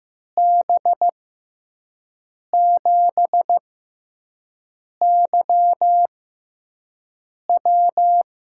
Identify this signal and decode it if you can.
Morse code, heard as a keyed tone, about 15 wpm, spaced as for 5 wpm, 700 Hz. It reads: B7YW
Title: USA Call Signs - 6 Characters - 6x Character Spacing - 15wpm